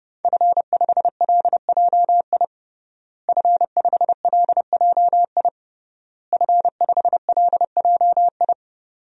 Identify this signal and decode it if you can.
Morse code, heard as a keyed tone, about 30 wpm, 700 Hz. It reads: F5LJI F5LJI F5LJI
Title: DXCC-100 Call Signs - 5 Characters - Repeated 3x - 30wpm